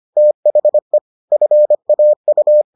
Morse code, heard as a keyed tone, about 25 wpm, 600 Hz. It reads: THE FAU